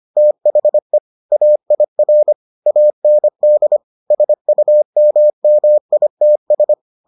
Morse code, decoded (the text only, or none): THE AIR AND SUMMITS